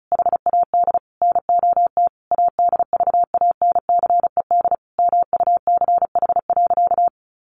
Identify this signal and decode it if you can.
Morse code, heard as a keyed tone, about 34 wpm, 700 Hz. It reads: HAD NOT ADVANCED MUCH.
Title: Sentences from Top 1200 Words 35wpm